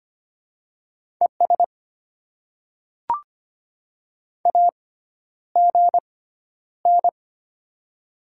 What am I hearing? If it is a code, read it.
Morse code, heard as a keyed tone, about 25 wpm, spaced as for 11 wpm, 700 Hz. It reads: ES AGN
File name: Top 20 QSO Elements - 6x Character Spacing - 25wpm